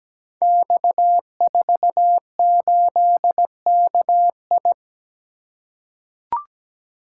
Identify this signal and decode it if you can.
Morse code, heard as a keyed tone, about 17 wpm, 700 Hz. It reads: X48KI
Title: Code Groups - 5 Letter:Numbers - 17wpm